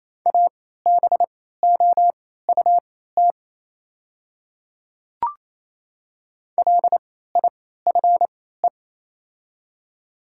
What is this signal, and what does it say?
Morse code, heard as a keyed tone, about 28 wpm, spaced as for 16 wpm, 700 Hz. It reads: ABOUT LIFE
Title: 3x Character Spacing - Top 500 Words - 28wpm